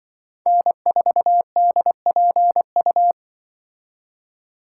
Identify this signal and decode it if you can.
Morse code, heard as a keyed tone, about 24 wpm, 700 Hz. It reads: N4DPU
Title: Speed Racing - USA 1x3 Call Signs 20wpm